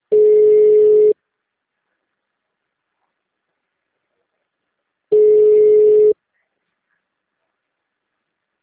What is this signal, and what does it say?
Ringback tone (Europe, 425 Hz).
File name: out- -2005- - - .3504